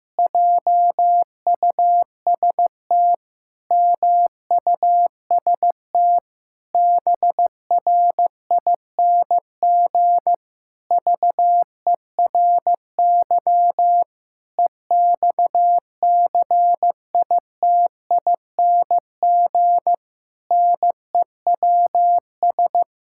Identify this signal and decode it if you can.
Morse code, heard as a keyed tone, about 15 wpm, 700 Hz. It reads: JUST MUST BRING VERY EXCITING NEWS
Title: Pangram Challenge v2 - 15wpm